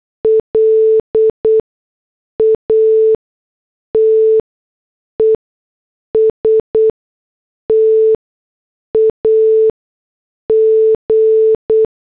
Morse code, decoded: LATEST AG